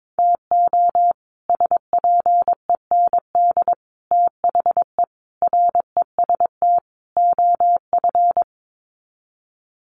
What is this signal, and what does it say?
Morse code, heard as a keyed tone, about 22 wpm, 700 Hz. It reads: TO SPEND THE REST OF
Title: Sets of 5 Words Encore2 22wpm